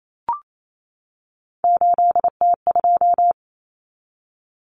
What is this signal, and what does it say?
Morse code, heard as a keyed tone, about 28 wpm, 700 Hz. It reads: 8T2
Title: Common Cut Numbers - 3-Digit 28wpm